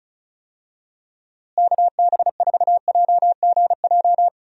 Morse code, heard as a keyed tone, about 36 wpm, 700 Hz. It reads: KB4JGJ
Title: USA Call Signs - 6 Characters - Repeated 3x - 35wpm